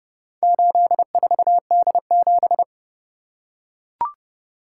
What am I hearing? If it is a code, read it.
Morse code, heard as a keyed tone, about 31 wpm, 700 Hz. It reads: 84D7